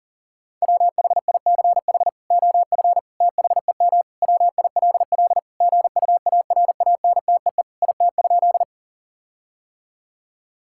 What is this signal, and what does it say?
Morse code, heard as a keyed tone, about 37 wpm, 700 Hz. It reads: WHICH OF THEM WILL GUARANTEE IT?
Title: Sentences from Top 1600 Words 40wpm